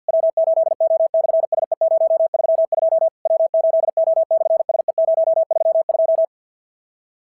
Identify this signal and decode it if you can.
Morse code, heard as a keyed tone, about 50 wpm, 650 Hz. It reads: W8O/SE032 W8O/SE032
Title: SOTA QSO Practice 50wpm